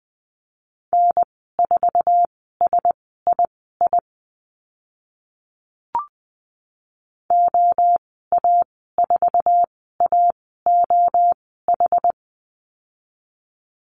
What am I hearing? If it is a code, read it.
Morse code, heard as a keyed tone, about 20 wpm, spaced as for 14 wpm, 700 Hz. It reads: N4SII OA4AOH
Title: 069 Lesson 12g - Letter d - Call Signs 20wpm